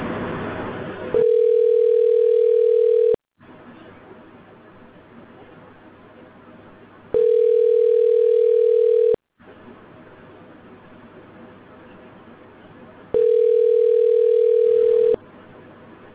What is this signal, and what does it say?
Ringback tone (North America), 440 + 480 Hz.